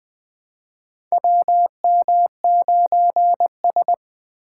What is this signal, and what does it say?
Morse code, heard as a keyed tone, about 21 wpm, 700 Hz. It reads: WM9S
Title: USA Call Signs - 4 Characters - 3x Character Spacing - 20wpm